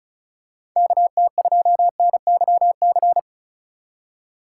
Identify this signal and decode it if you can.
Morse code, heard as a keyed tone, about 36 wpm, 700 Hz. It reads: KT2NYC
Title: USA Call Signs - 6 Characters - 3x Character Spacing - 35wpm